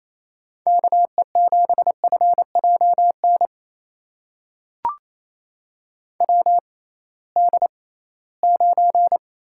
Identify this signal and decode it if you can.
Morse code, heard as a keyed tone, about 28 wpm, spaced as for 18 wpm, 700 Hz. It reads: KE7FJN WD9
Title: USA Call Signs - 6 Characters - 6x Character Spacing - 28wpm